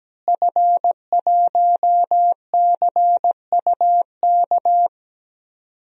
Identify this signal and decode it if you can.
Morse code, heard as a keyed tone, about 17 wpm, 700 Hz. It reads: F1CUK
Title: DXCC-100 Call Signs - 5 Characters - Repeated 3x - 17wpm